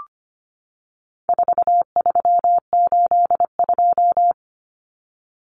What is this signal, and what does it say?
Morse code, heard as a keyed tone, about 25 wpm, 700 Hz. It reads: 4382
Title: Common Cut Numbers - 4-Digit 25wpm